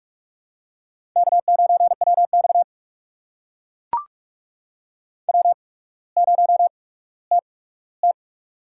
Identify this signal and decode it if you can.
Morse code, heard as a keyed tone, about 44 wpm, spaced as for 19 wpm, 700 Hz. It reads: K9WX W0TT
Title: USA Call Signs - 4 Characters - 8x Character Spacing - 45wpm